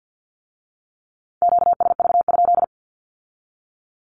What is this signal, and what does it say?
Morse code, heard as a keyed tone, about 50 wpm, 700 Hz. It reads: QSV?
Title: Q-Codes with Spoken Meaning - 50wpm